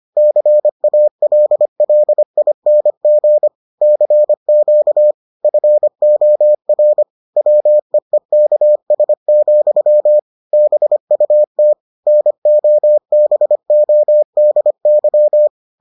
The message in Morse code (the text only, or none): CALLING CQ FOR WEEKS, BUT NOBODY